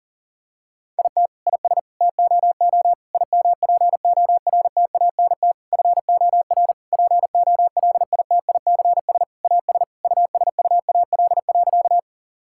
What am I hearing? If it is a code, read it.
Morse code, heard as a keyed tone, about 37 wpm, 700 Hz. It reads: IT IS TOO IMPORTANT FOR POLITICS AS USUAL.
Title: Sentences from Top 1000 Words - Review Part 6 - 40wpm